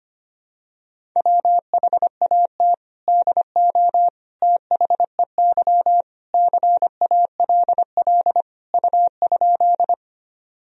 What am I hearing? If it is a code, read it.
Morse code, heard as a keyed tone, about 25 wpm, 700 Hz. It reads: WHAT DO THEY CALL U?
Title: Sentences from Top 100 Words and Abbreviations 25wpm